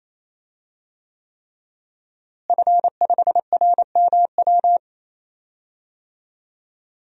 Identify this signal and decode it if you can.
Morse code, heard as a keyed tone, about 28 wpm, 700 Hz. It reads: F5RMW
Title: DXCC-100 Call Signs - 5 Characters - Repeated 3x - 28wpm